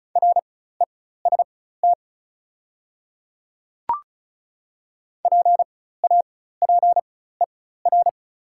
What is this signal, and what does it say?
Morse code, heard as a keyed tone, about 36 wpm, spaced as for 16 wpm, 690 Hz. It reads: REST PAPER
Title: 4x Character Spacing - Top 500 Words - 35wpm